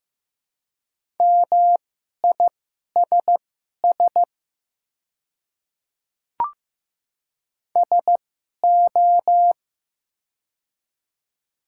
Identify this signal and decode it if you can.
Morse code, heard as a keyed tone, about 15 wpm, spaced as for 11 wpm, 700 Hz. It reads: MISS SO
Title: 2x Character Spacing - Top 500 Words - 15wpm